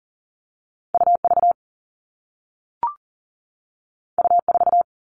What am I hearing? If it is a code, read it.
Morse code, heard as a keyed tone, about 40 wpm, 700 Hz. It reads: UV U4
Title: All Cut Numbers - 2-Digit 40wpm